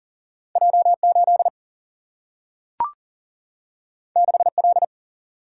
Morse code, heard as a keyed tone, about 40 wpm, 700 Hz. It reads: J8 6L
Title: Single-Letter Single-Number 40wpm